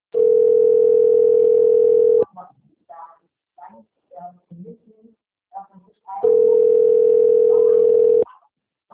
Ringback tone (North America), 440 + 480 Hz.